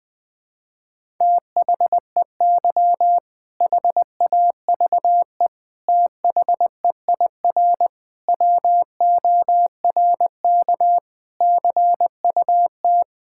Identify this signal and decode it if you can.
Morse code, heard as a keyed tone, about 21 wpm, 700 Hz. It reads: THEY HAVE THEIR WORK CUT
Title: Sentences with 7-Word Phrases - 20wpm